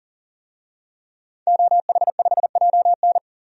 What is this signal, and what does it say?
Morse code, heard as a keyed tone, about 40 wpm, 700 Hz. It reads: OH5JN
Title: DXCC-100 Call Signs - 5 Characters - 40wpm